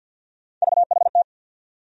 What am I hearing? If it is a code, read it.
Morse code, heard as a keyed tone, about 49 wpm, 700 Hz. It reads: VHT
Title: USA Call Sign Postfixes - 3 Letters 50wpm